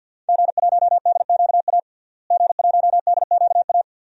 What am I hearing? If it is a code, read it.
Morse code, heard as a keyed tone, about 52 wpm, 700 Hz. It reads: G1DQA G1DQA
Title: DXCC-100 Call Signs - 5 Characters - Repeated 2x - 50wpm